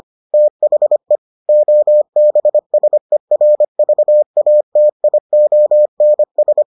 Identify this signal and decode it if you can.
Morse code, heard as a keyed tone, about 25 wpm, 600 Hz. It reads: THE OBSERVATIONS